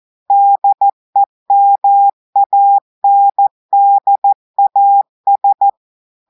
Morse code, heard as a keyed tone, about 14 wpm, 800 Hz. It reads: DEMANDAS
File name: audios / portugais / portugais texte en clair 2 vitesse 14 mots minutes